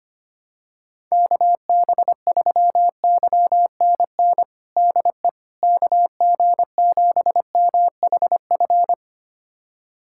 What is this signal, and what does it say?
Morse code, heard as a keyed tone, about 25 wpm, 700 Hz. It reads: KB3YNN DE KG7MHF
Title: USA Callsign DE Callsign - 6 Characters - 25wpm